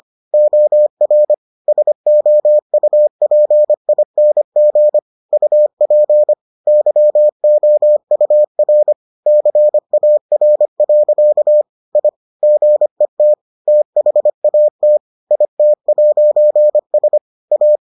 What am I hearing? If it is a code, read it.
Morse code, heard as a keyed tone, about 25 wpm, 600 Hz. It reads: OR SOUPING UP YOUR CAR. I GET THAT IT'S A